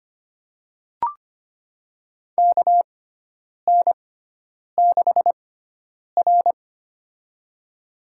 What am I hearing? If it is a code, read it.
Morse code, heard as a keyed tone, about 25 wpm, spaced as for 9 wpm, 700 Hz. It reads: KN6R
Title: USA Call Signs - 4 Characters - 6x Character Spacing - 25wpm